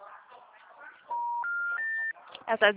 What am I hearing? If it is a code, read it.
Special information tone, with speech.